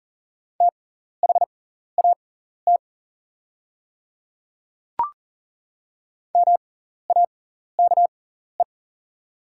Morse code, heard as a keyed tone, about 40 wpm, spaced as for 14 wpm, 700 Hz. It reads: THAT MAKE
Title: 6x Character Spacing - Top 300 Words - 40wpm